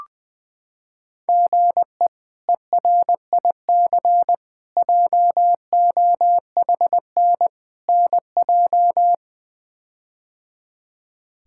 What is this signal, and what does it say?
Morse code, heard as a keyed tone, about 21 wpm, 700 Hz. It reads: GE ERIC JOHN NJ
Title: K1USN SST QSO Practice - 20wpm